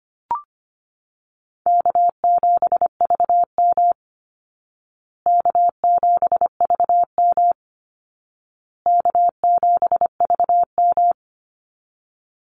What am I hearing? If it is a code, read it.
Morse code, heard as a keyed tone, about 25 wpm, 700 Hz. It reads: K7VM K7VM K7VM
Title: Call Signs Repeated 2x - 25wpm